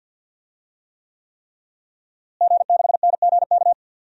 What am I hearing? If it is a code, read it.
Morse code, heard as a keyed tone, about 52 wpm, 700 Hz. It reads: G6NGK
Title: DXCC-100 Call Signs - 5 Characters - Repeated 3x - 50wpm